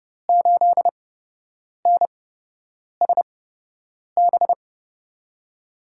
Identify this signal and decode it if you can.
Morse code, heard as a keyed tone, about 31 wpm, spaced as for 8 wpm, 700 Hz. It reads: 8NSB
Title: USA Call Signs - 5 Characters - 8x Character Spacing - 30wpm